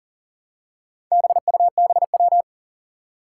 Morse code, heard as a keyed tone, about 37 wpm, 700 Hz. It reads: BUBW
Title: Four Challenging Letter-Number Combinations 40wpm